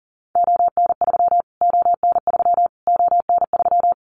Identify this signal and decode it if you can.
Morse code, heard as a keyed tone, about 37 wpm, 700 Hz. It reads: ON3 ON3 ON3
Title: Callsign Prefixes for Top 25 DXCC - Repeated 3x - 40wpm